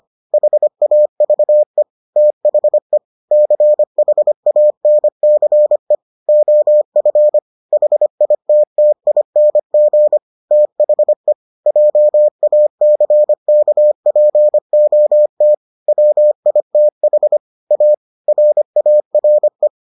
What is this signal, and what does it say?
Morse code, heard as a keyed tone, about 25 wpm, 600 Hz. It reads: HAVE THE CHANCE OF HITTING THE JACKPOT WITH A RARE